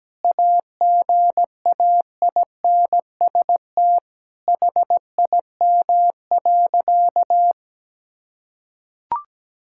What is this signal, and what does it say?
Morse code, heard as a keyed tone, about 17 wpm, 700 Hz. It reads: AGAINST HIM.
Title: Sentences from Top 1600 Words 17wpm